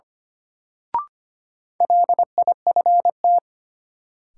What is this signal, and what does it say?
Morse code, heard as a keyed tone, about 25 wpm, 700 Hz. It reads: LIFT
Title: Sentences from Top 1000 Words - New Word Drill - 25wpm